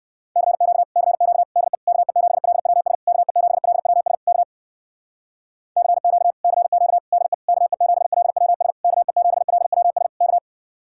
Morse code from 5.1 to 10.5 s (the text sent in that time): CQ CQ DE KE7LWS KE7LWS K